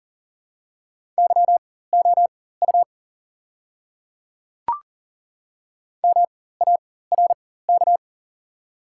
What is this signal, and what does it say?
Morse code, heard as a keyed tone, about 40 wpm, spaced as for 19 wpm, 700 Hz. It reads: YOU MARK
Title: 4x Character Spacing - Top 500 Words - 40wpm